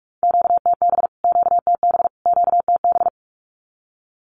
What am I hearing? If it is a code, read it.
Morse code, heard as a keyed tone, about 44 wpm, 700 Hz. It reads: QTB QTB QTB
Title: Q-Codes Repeated 3x 45wpm